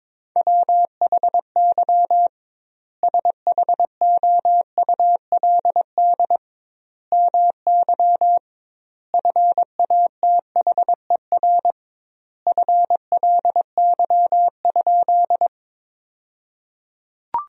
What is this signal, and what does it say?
Morse code, heard as a keyed tone, about 22 wpm, 700 Hz. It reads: WHY SHOULD MY FATHER FLY?
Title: 2x Word-Spacing - Sentences from Top 500 Words - 22wpm